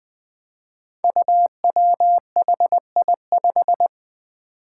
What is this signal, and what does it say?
Morse code, heard as a keyed tone, about 20 wpm, 700 Hz. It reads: UWHI5
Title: Five Challenging Letter-Number Combinations 20wpm